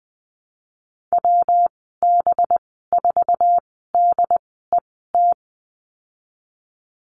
Morse code, heard as a keyed tone, about 20 wpm, spaced as for 14 wpm, 700 Hz. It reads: WB4DET
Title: 186 Lesson 30g - Letter b - Call Signs 20wpm